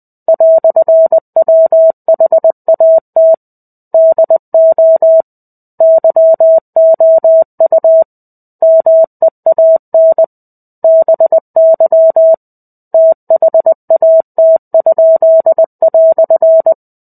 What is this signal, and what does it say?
Morse code, heard as a keyed tone, about 20 wpm, 650 Hz. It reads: "WHAT DO YOU MEAN BY THAT?"
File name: 05-Chapter-05 20 WPM